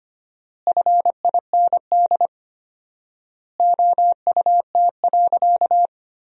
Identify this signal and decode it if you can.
Morse code, heard as a keyed tone, about 25 wpm, 700 Hz. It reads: FIND OUT.
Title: 4x Word-Spacing - Sentences from Top 200 Words - 25wpm